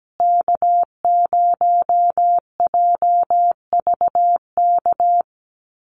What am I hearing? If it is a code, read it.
Morse code, heard as a keyed tone, about 17 wpm, 700 Hz. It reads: K0JVK
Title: USA Call Signs Repeated 3x 17wpm